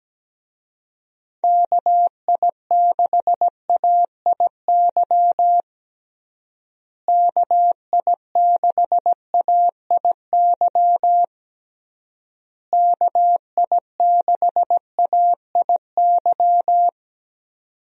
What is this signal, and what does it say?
Morse code, heard as a keyed tone, about 17 wpm, 700 Hz. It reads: KI6AIY KI6AIY KI6AIY
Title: USA Call Signs - 6 Characters - Repeated 3x - 17wpm